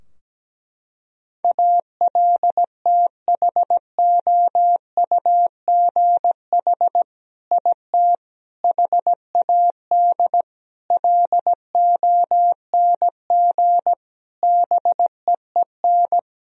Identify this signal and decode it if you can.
Morse code, heard as a keyed tone, about 17 wpm, 700 Hz. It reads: ALTHOUGH IT HAD LONG BEEN
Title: Sentences from Top 1000 Words - Review Part 3 - 17wpm